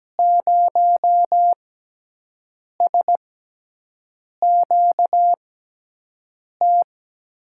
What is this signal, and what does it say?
Morse code, heard as a keyed tone, about 17 wpm, spaced as for 6 wpm, 700 Hz. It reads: 0SQT
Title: USA Call Signs - 6 Characters - 6x Character Spacing - 17wpm